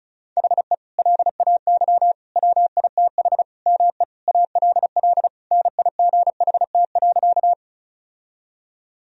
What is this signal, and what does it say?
Morse code, heard as a keyed tone, about 36 wpm, 700 Hz. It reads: HE LAY WITH ME ALL NIGHT.
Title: Sentences from Top 1000 Words - Review Part 2 - 35wpm